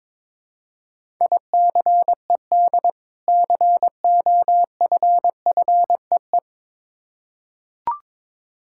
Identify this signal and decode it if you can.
Morse code, heard as a keyed tone, about 22 wpm, 700 Hz. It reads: ICED COFFEE
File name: practice / 22wpm / Difficult Words with Hints - 22wpm